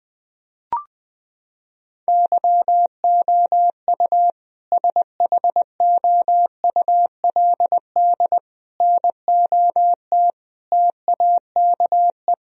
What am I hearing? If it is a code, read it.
Morse code, heard as a keyed tone, about 20 wpm, 700 Hz. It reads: YOU SHOULD NOT TAKE
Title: Sentences from Top 1000 Words - Review Part 3 - 20wpm